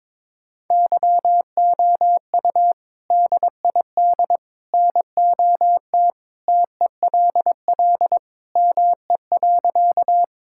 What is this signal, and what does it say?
Morse code, heard as a keyed tone, about 22 wpm, 700 Hz. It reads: YOU DID NOT TELL ME.
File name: practice / 22wpm / Sentences with 3-Word Phrases - 22wpm